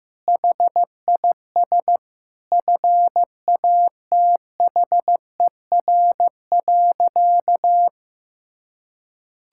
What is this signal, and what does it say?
Morse code, heard as a keyed tone, about 14 wpm, 700 Hz. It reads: HIS FATHER.
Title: Sentences from Top 1000 Words - Review Part 5 - 15wpm